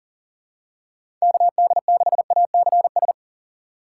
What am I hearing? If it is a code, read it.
Morse code, heard as a keyed tone, about 40 wpm, 700 Hz. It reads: KD6ACS
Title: USA Call Signs - 6 Characters - 4x Character Spacing - 40wpm